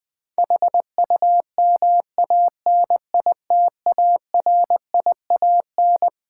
Morse code, heard as a keyed tone, about 21 wpm, 700 Hz. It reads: HUMANITARIAN
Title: Build A Word - 6 Letters At A Time - 12-Letter Words - 20wpm